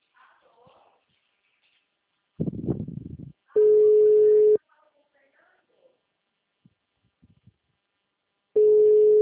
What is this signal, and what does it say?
Ringback tone (Europe, 425 Hz).
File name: out- -2048- - - .27753